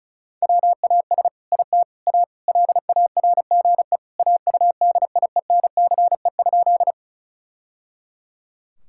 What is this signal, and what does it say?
Morse code, heard as a keyed tone, about 36 wpm, 700 Hz. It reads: WAS IT A LARGE AUDIENCE?
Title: Sentences from Top 1000 Words - Review Part 5 - 35wpm